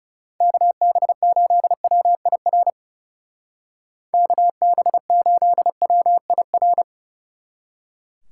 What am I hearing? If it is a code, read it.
Morse code, heard as a keyed tone, about 34 wpm, 700 Hz. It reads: KB8WIR KB8WIR
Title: Speed Racing - USA 2x3 Call Signs 30wpm